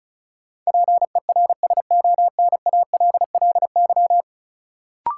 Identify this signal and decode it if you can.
Morse code, heard as a keyed tone, about 36 wpm, 700 Hz. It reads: PERSONALLY
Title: Sentences from Top 1400 Words - New Word Drill Alt - 35wpm